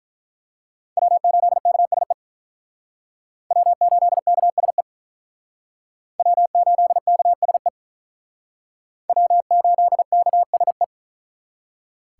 Morse code, heard as a keyed tone, about 43 wpm, 700 Hz. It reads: W8KSE W8KSE W8KSE W8KSE
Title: Speed Racing - USA 1x3 Call Signs 35wpm